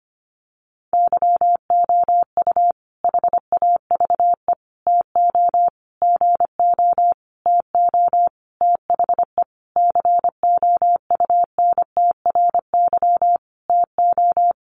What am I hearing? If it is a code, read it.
Morse code, heard as a keyed tone, about 25 wpm, 700 Hz. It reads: YOU HAVE TO GO TO THE COUNTRY TO
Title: Sentences from Top 1300 Words 25wpm